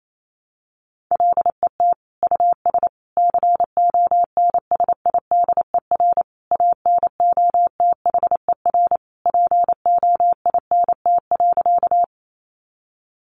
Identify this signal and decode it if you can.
Morse code, heard as a keyed tone, about 28 wpm, 700 Hz. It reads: LET US CONSIDER ANOTHER POINT.
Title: Sentences from Top 600 Words 28wpm